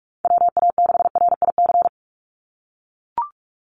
Morse code, heard as a keyed tone, about 44 wpm, 700 Hz. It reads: WA6RIC
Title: NA SOTA Chaser Callsigns 45wpm